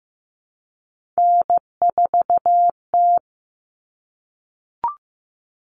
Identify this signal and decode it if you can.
Morse code, heard as a keyed tone, about 15 wpm, 700 Hz. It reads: N4T
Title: All Cut Numbers - 3-Digit 15wpm